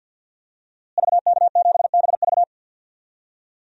Morse code, heard as a keyed tone, about 47 wpm, 700 Hz. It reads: VK7BV
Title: DXCC-100 Call Signs - 5 Characters - Repeated 2x - 50wpm